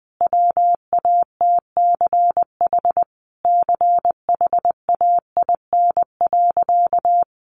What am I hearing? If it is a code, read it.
Morse code, heard as a keyed tone, about 20 wpm, 700 Hz. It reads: WATCH CHAIN.
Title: Sentences from Top 1500 Words 20wpm